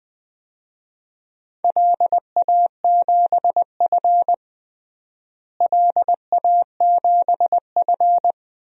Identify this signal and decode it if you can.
Morse code, heard as a keyed tone, about 20 wpm, 700 Hz. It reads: LA7F LA7F
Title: DXCC-100 Call Signs - 4 Characters - Repeated 2x - 20wpm